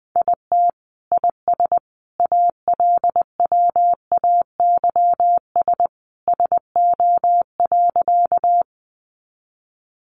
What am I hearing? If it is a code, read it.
Morse code, heard as a keyed tone, about 20 wpm, 700 Hz. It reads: IT IS ALWAYS SO.
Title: Sentences from Top 300 Words 20wpm